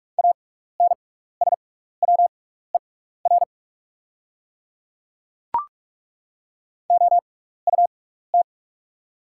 Morse code, heard as a keyed tone, about 43 wpm, spaced as for 15 wpm, 690 Hz. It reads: ANSWER OUT
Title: 6x Character Spacing - Top 300 Words - 45wpm